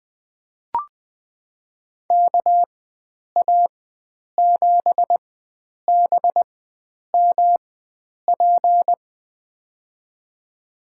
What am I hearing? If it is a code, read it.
Morse code, heard as a keyed tone, about 20 wpm, spaced as for 9 wpm, 700 Hz. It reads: KA7BMP
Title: USA Call Signs - 6 Characters - 4x Character Spacing - 20wpm